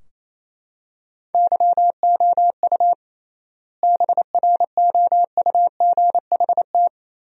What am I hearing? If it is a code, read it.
Morse code, heard as a keyed tone, about 28 wpm, 700 Hz. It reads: YOU BROUGHT
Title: 3x Word-Spacing - Sentences from Top 500 Words - 28wpm